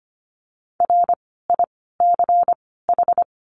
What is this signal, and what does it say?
Morse code, heard as a keyed tone, about 25 wpm, 700 Hz. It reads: R I C H
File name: course / 25:10wpm / 094 Lesson 16f - Letter c - Words 25wpm